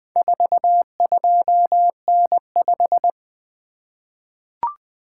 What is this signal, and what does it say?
Morse code, heard as a keyed tone, about 20 wpm, 700 Hz. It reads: 42N5
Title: Four Letter-Number Combinations - 20wpm